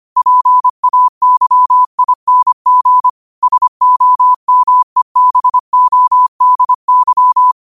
Morse code, heard as a keyed tone, about 25 wpm, 1000 Hz. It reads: PAYING SOMEBODY
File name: qotdfiles / 25wpm